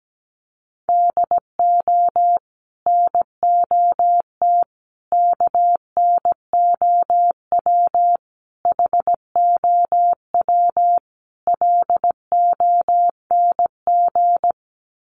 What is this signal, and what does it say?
Morse code, heard as a keyed tone, about 17 wpm, 700 Hz. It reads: DO NOT KNOW HOW LONG
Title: Sets of 5 Words Encore2 17wpm